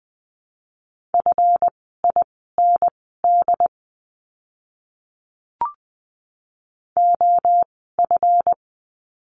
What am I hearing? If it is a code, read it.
Morse code, heard as a keyed tone, about 20 wpm, spaced as for 15 wpm, 700 Hz. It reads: FIND OF
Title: 146 Lesson 24f - Letter y - Words 20wpm